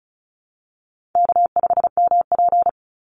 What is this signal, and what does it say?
Morse code, heard as a keyed tone, about 34 wpm, 700 Hz. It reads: K5MP
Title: Call Signs Repeated 2x - 35wpm